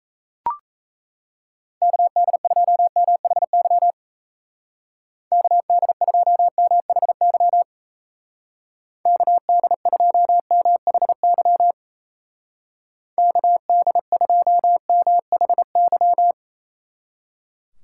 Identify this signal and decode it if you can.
Morse code, heard as a keyed tone, about 35 wpm, 700 Hz. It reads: KD2MHY KD2MHY KD2MHY KD2MHY